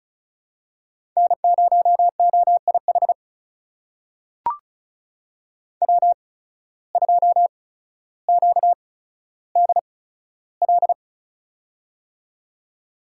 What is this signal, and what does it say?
Morse code, heard as a keyed tone, about 36 wpm, spaced as for 15 wpm, 700 Hz. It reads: N0OIH W2QDL
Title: USA Call Signs - 5 Characters - 8x Character Spacing - 35wpm